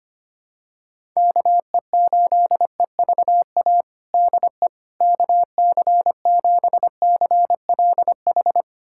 Morse code, heard as a keyed tone, about 25 wpm, 700 Hz. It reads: KE8EVA DE KC7CLH
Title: USA Callsign DE Callsign 25wpm